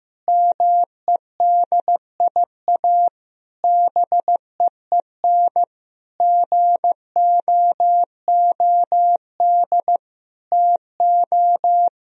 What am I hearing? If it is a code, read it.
Morse code, heard as a keyed tone, about 15 wpm, 700 Hz. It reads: MEDIA BEEN GOOD TO
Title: Sentences from Top 1000 Words - Review Part 4 - 15wpm